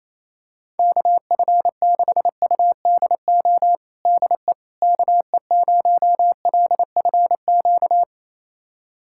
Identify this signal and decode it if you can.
Morse code, heard as a keyed tone, about 28 wpm, 700 Hz. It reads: KF6UDO DE KE0LFQ
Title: USA Callsign DE Callsign 28wpm